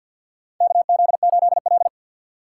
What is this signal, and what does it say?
Morse code, heard as a keyed tone, about 51 wpm, 700 Hz. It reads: KZ8L
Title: USA Call Signs - 4 Characters - 2x Character Spacing - 50wpm